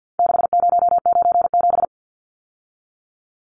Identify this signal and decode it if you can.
Morse code, heard as a keyed tone, about 48 wpm, 700 Hz. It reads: 6097
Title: Numbers - 4-Digit 50wpm